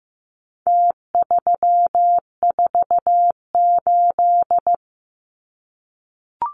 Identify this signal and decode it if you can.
Morse code, heard as a keyed tone, about 15 wpm, 700 Hz. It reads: T348
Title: Numbers - Contest Style - 4-Digit 15wpm